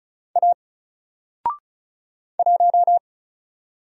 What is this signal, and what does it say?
Morse code, heard as a keyed tone, about 33 wpm, 700 Hz. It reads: A 1
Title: Single Letter-Number 35wpm